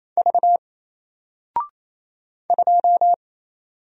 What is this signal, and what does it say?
Morse code, heard as a keyed tone, about 27 wpm, 700 Hz. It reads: V 2
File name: Single-Letter-Number-28wpm